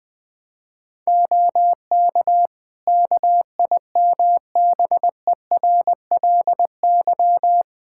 Morse code, heard as a keyed tone, about 21 wpm, 700 Hz. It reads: OK KIMBERLY
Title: Practice QSO - Exchange 4 - Ending - 20wpm